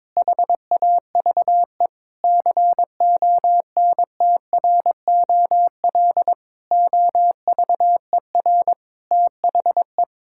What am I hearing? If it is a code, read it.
Morse code, heard as a keyed tone, about 22 wpm, 700 Hz. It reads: HAVE CONTROL OVER THE
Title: Sentences from Top 1000 Words - Review Part 5 - 22wpm